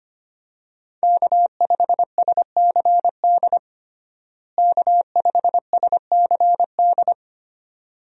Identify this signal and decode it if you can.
Morse code, heard as a keyed tone, about 25 wpm, 700 Hz. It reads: K5SCD K5SCD
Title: USA Call Signs - 5 Characters - Repeated 2x - 25wpm